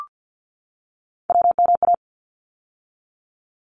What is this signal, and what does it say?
Morse code, heard as a keyed tone, about 50 wpm, 700 Hz. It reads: WMA
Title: ARRL Sections 50wpm